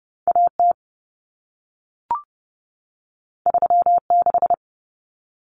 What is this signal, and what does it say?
Morse code, heard as a keyed tone, about 30 wpm, 700 Hz. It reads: AT 36
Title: Common Cut Numbers - 2-Digit 30wpm